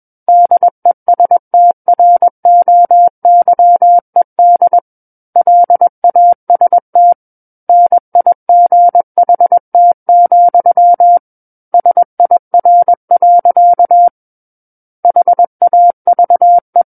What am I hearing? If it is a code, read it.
Morse code, heard as a keyed tone, about 21 wpm, 700 Hz. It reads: DESTROYED LAST NIGHT, SIR. HAVE
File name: Book One / 12-Chapter 12 - 21 WPM